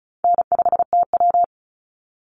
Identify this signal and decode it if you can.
Morse code, heard as a keyed tone, about 34 wpm, 700 Hz. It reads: N5TW
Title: Easy Call Signs Repeated 3x - 35wpm